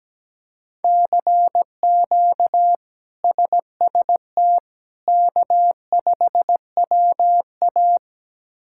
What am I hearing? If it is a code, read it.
Morse code, heard as a keyed tone, about 17 wpm, 700 Hz. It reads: CQ SST K5WA
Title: K1USN SST QSO Practice - 17wpm